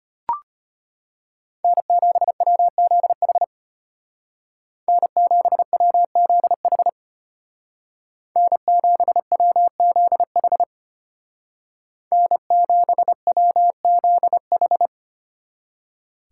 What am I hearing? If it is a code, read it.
Morse code, heard as a keyed tone, about 32 wpm, 700 Hz. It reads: N7WZH N7WZH N7WZH N7WZH